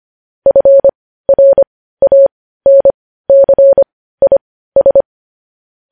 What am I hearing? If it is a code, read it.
Morse code, heard as a keyed tone, about 25 wpm, spaced as for 15 wpm, 550 Hz. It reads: FRANCIS
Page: news-headlines-in-morse-code-at-15-wpm